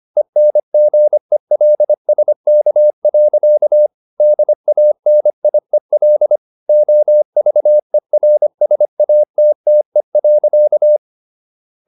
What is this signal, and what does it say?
Morse code, heard as a keyed tone, about 25 wpm, 600 Hz. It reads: ENGELSK. DANIEL OVERSATTE.
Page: Ferie 24 wpm